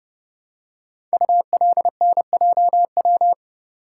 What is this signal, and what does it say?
Morse code, heard as a keyed tone, about 28 wpm, 700 Hz. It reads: ULNJW
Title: Code Groups - 5 Letters - 30wpm